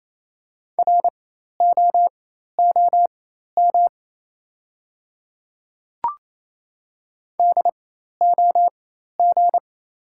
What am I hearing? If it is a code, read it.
Morse code, heard as a keyed tone, about 28 wpm, spaced as for 13 wpm, 700 Hz. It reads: ROOM DOG
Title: 4x Character Spacing - Top 500 Words - 28wpm